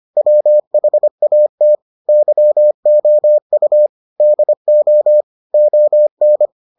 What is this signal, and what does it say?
Morse code, heard as a keyed tone, about 25 wpm, 600 Hz. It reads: WHAT YOU DO ON